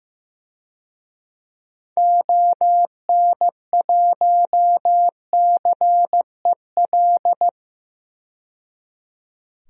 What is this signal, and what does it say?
Morse code, heard as a keyed tone, about 15 wpm, 700 Hz. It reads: ON1CEL
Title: DXCC-100 Call Signs - 6 Characters - 15wpm